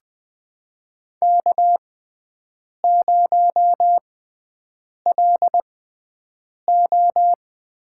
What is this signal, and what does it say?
Morse code, heard as a keyed tone, about 20 wpm, spaced as for 7 wpm, 700 Hz. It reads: K0LO
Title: USA Call Signs - 6x Character Spacing - 20wpm